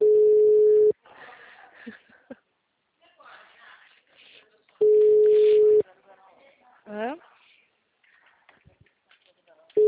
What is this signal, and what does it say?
Ringback tone (Europe, 425 Hz), with speech.